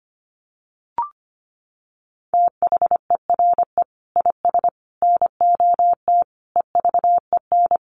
Morse code, heard as a keyed tone, about 25 wpm, 700 Hz. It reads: THERE IS NOT EVEN
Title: Sentences from Top 1500 Words 25wpm